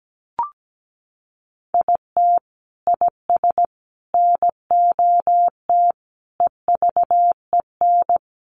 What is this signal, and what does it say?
Morse code, heard as a keyed tone, about 17 wpm, 700 Hz. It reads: IT IS NOT EVEN
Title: Sentences from Top 1500 Words 17wpm